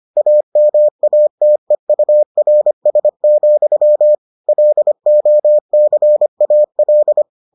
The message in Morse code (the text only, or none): AMATEURS, LOCAL